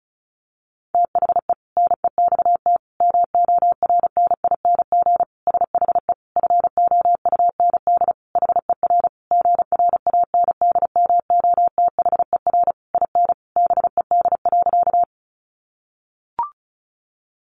Morse code, heard as a keyed tone, about 34 wpm, 700 Hz. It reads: THE NEXT MORNING SHE FOUND HER GRANDMOTHER IN BED.
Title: Sentences from Top 1200 Words 35wpm